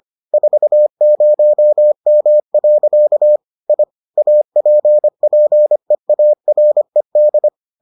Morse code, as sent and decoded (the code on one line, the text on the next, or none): ....- ----- -- .-.-.- / .. / .- .--. .--. .- .-. -..
40M. I APPEARED